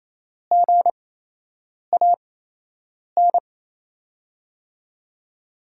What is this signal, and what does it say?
Morse code, heard as a keyed tone, about 27 wpm, spaced as for 8 wpm, 700 Hz. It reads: GAN
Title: USA Call Signs - 5 Characters - 8x Character Spacing - 28wpm